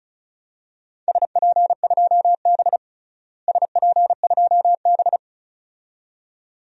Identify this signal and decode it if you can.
Morse code, heard as a keyed tone, about 34 wpm, 700 Hz. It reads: SP2B SP2B
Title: DXCC-100 Call Signs - 4 Characters - Repeated 2x - 35wpm